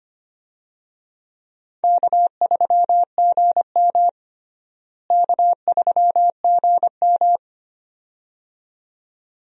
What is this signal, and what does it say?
Morse code, heard as a keyed tone, about 25 wpm, 700 Hz. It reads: K3GM K3GM
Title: USA Call Signs - 4 Characters - Repeated 2x - 25wpm